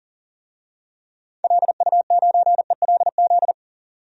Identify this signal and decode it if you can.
Morse code, heard as a keyed tone, about 40 wpm, 700 Hz. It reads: LU9ELZ
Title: DXCC-100 Call Signs - 6 Characters - 40wpm